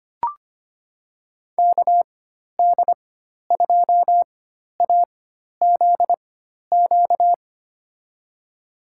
Morse code, heard as a keyed tone, about 25 wpm, spaced as for 12 wpm, 700 Hz. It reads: KD2AZQ
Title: USA Call Signs - 6 Characters - 4x Character Spacing - 25wpm